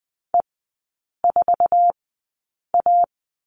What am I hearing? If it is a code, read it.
Morse code, heard as a keyed tone, about 20 wpm, spaced as for 8 wpm, 700 Hz. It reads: E4A
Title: 053 Lesson 10e - Letter r - Three Characters 20wpm